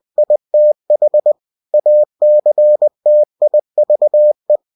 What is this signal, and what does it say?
Morse code, heard as a keyed tone, about 20 wpm, 600 Hz. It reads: ITH ACTIVE
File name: cw-books / winnie-the-pooh / winnie-20wpm-page-0116